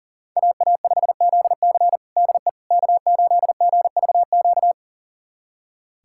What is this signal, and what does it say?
Morse code, heard as a keyed tone, about 40 wpm, 700 Hz. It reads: AA5ZC DE K8GVQ
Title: USA Callsign DE Callsign - 5 Characters - 40wpm